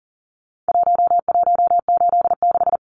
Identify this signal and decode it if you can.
Morse code, heard as a keyed tone, about 40 wpm, 700 Hz. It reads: 1186
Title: Numbers - Contest Style - 4-Digit 40wpm